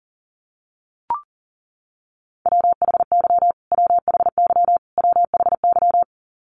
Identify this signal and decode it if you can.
Morse code, heard as a keyed tone, about 40 wpm, 700 Hz. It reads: WHY WHY WHY
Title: Top 200 Words Repeated 40wpm